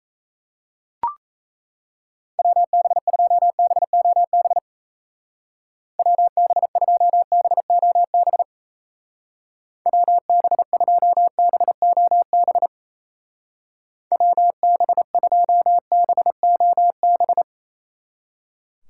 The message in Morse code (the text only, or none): WB2BOB WB2BOB WB2BOB WB2BOB